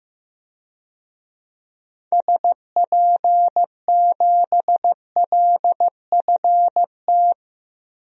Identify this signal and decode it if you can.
Morse code, heard as a keyed tone, about 15 wpm, 700 Hz. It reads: SP7LFT
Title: DXCC-100 Call Signs - 6 Characters - 15wpm